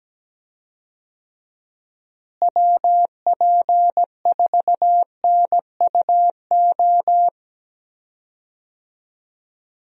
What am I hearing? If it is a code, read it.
Morse code, heard as a keyed tone, about 17 wpm, 700 Hz. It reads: WP4NUO
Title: USA Call Signs - 6 Characters - Repeated 3x - 17wpm